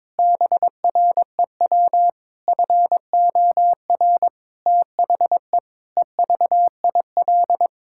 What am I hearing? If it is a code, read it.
Morse code, heard as a keyed tone, about 22 wpm, 700 Hz. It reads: BREW FOR THE EVIL
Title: Pangram Challenge v2 - 22wpm